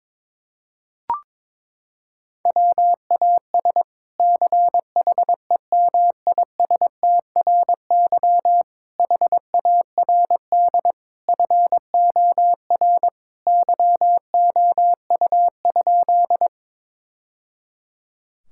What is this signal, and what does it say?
Morse code, heard as a keyed tone, about 22 wpm, 700 Hz. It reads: WAS CHEMISTRY HARD FOR YOU?
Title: Sentences from Top 1000 Words - Review Part 6 - 22wpm